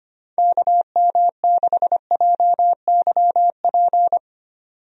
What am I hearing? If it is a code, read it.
Morse code, heard as a keyed tone, about 25 wpm, 700 Hz. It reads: KM6JYP
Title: USA Call Signs - 6 Characters - 25wpm